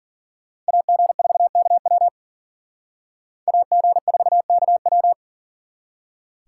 Morse code, heard as a keyed tone, about 52 wpm, 700 Hz. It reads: AG4KW AG4KW